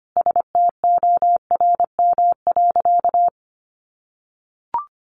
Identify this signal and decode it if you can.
Morse code, heard as a keyed tone, about 25 wpm, 700 Hz. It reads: STORM.
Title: Sentences from Top 1500 Words 25wpm